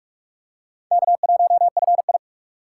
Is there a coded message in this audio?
Yes: Morse code from 0.9 to 2.2 s, heard as a keyed tone, about 44 wpm, 700 Hz. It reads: K1FI